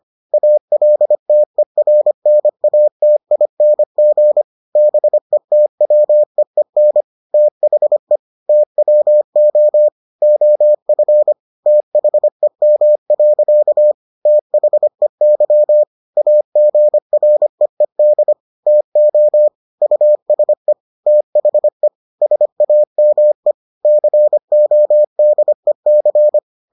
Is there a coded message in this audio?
Yes: Morse code from 0.3 to 26.4 s, heard as a keyed tone, about 25 wpm, 600 Hz. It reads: ALTERNATING BETWEEN THE TWO OF THEM. THEY AGREED TO USE THE SAME CODEC